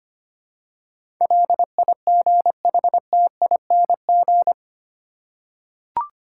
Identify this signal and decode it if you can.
Morse code, heard as a keyed tone, about 25 wpm, 700 Hz. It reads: LIGHTING
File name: 8 Letter Words - First 3 Letters Given - 25wpm